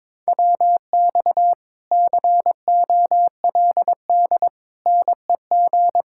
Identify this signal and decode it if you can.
Morse code, heard as a keyed tone, about 22 wpm, 700 Hz. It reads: WX COLD NEG
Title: Speed-Racing Common QSO Phrases Encore - 22wpm